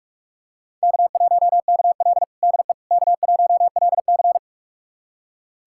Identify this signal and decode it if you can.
Morse code, heard as a keyed tone, about 46 wpm, 700 Hz. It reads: K1KL DE K1LC
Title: USA Callsign DE Callsign - 4 Characters - 45wpm